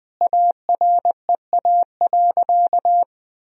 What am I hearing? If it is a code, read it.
Morse code, heard as a keyed tone, about 21 wpm, 700 Hz. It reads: AREA.
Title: Sentences from Top 1000 Words - Review Part 2 - 20wpm